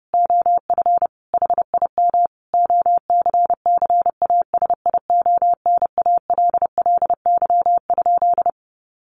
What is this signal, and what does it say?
Morse code, heard as a keyed tone, about 30 wpm, 700 Hz. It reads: OF HIM OCCASIONALLY?
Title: Sentences from Top 1300 Words 30wpm